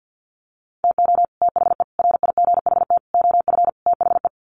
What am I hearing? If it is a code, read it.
Morse code, heard as a keyed tone, about 50 wpm, 700 Hz. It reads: TO THE RIGHT OF THE
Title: Sets of 5 Words Encore 50wpm